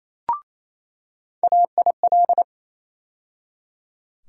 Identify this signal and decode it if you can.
Morse code, heard as a keyed tone, about 28 wpm, 700 Hz. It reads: AIL
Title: USA Call Sign Postfixes - 3 Letters 28wpm